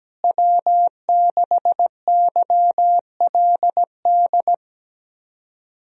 Morse code, heard as a keyed tone, about 17 wpm, 700 Hz. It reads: W6YLD
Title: USA Call Signs - 5 Characters - Repeated 3x - 17wpm